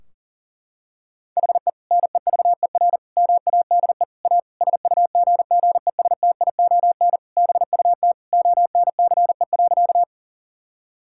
Morse code, heard as a keyed tone, about 40 wpm, 700 Hz. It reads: HE NEVER MADE A SUGGESTION BUT ONCE.
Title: Sentences from Top 1600 Words 40wpm